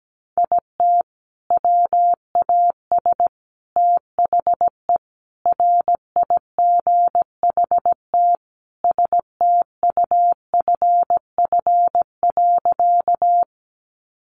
Morse code, heard as a keyed tone, about 17 wpm, 700 Hz. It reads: IT WAS THE RIGHT STUFF.
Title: Sentences from Top 1400 Words 17wpm